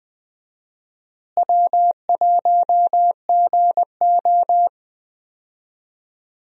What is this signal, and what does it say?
Morse code, heard as a keyed tone, about 20 wpm, 700 Hz. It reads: W1GO
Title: USA Call Signs - 4 Characters - Repeated 3x - 20wpm